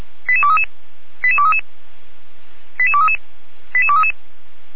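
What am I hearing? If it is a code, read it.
Five-tone selective call (ZVEI), heard as quick runs of five tones, tones of about 70 ms. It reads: 88233 88233 88233 88233
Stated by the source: ZVEI five-tone selective call received by SDR.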